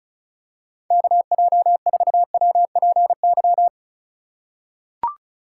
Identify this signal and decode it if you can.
Morse code, heard as a keyed tone, about 36 wpm, 700 Hz. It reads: KJ4WPY
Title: USA Call Signs - 6 Characters - 4x Character Spacing - 35wpm